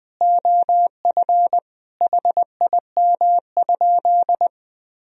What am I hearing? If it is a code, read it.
Morse code, heard as a keyed tone, about 21 wpm, 700 Hz. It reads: OF HIM?
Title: Sentences from Top 1000 Words - Review Part 4 - 20wpm